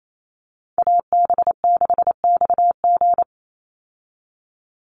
Morse code, heard as a keyed tone, about 28 wpm, 700 Hz. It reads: AB6XG
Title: Call Signs Repeated 2x - 28wpm